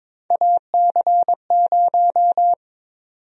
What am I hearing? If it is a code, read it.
Morse code, heard as a keyed tone, about 22 wpm, 700 Hz. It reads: AC0
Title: USA Call Sign Prefixes - 22wpm